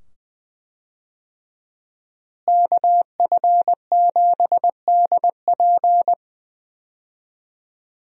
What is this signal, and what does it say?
Morse code, heard as a keyed tone, about 20 wpm, 700 Hz. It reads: KF7DP
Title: USA Call Signs - 5 Characters - Repeated 2x - 20wpm